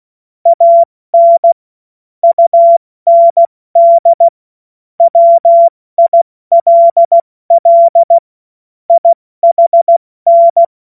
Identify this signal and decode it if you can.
Morse code, heard as a keyed tone, about 16 wpm, 680 Hz. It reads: AN UND WILL IHN
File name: Morsekurs / Witze / Witz 18. Januar 2022, 70 F80